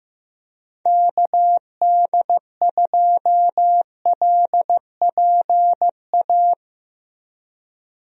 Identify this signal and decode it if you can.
Morse code, heard as a keyed tone, about 15 wpm, 700 Hz. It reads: KD2LPA
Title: USA Call Signs - 6 Characters - 15wpm